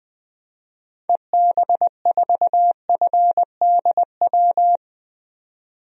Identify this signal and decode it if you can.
Morse code, heard as a keyed tone, about 20 wpm, 700 Hz. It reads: EB4FDW
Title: DXCC-100 Call Signs - 6 Characters - 20wpm